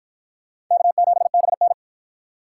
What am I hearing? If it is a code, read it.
Morse code, heard as a keyed tone, about 53 wpm, 700 Hz. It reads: K7BN